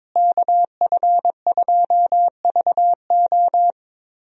Morse code, heard as a keyed tone, about 22 wpm, 700 Hz. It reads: KF2VO
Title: USA Call Signs - 5 Characters - Repeated 3x - 22wpm